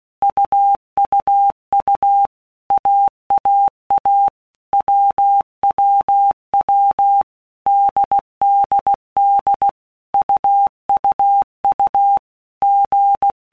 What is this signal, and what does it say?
Morse code, heard as a keyed tone, about 16 wpm, 800 Hz. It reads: UUU AAA WWW DDD UUU G
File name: All Groups 1-minute Blocks / Groups@15 / AWUNDG